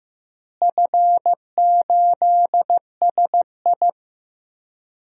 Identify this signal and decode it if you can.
Morse code, heard as a keyed tone, about 15 wpm, 700 Hz. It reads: F8SI
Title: DXCC-100 Call Signs - 4 Characters - Repeated 3x - 15wpm